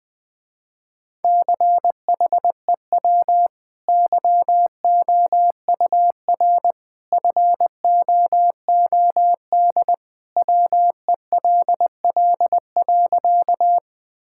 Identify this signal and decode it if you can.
Morse code, heard as a keyed tone, about 20 wpm, 700 Hz. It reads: CHEW YOUR FOOD WELL.
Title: Easy Sentences with Hints - 20wpm